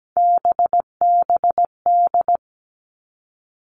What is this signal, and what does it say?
Morse code, heard as a keyed tone, about 17 wpm, 700 Hz. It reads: BBD
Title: All Cut Numbers - 3-Digit 17wpm